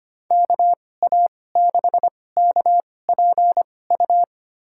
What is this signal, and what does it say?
Morse code, heard as a keyed tone, about 25 wpm, spaced as for 18 wpm, 700 Hz. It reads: KA6KPU
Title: USA Call Signs - 6 Characters - 2x Character Spacing - 25wpm